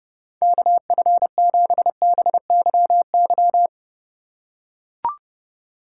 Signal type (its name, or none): Morse code